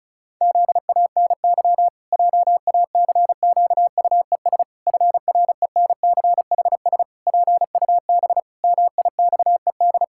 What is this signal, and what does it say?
Morse code, heard as a keyed tone, about 36 wpm, 700 Hz. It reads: ZANY JACQUES FRENCHS PUB MIXED
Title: Pangram Challenge v2 - 35wpm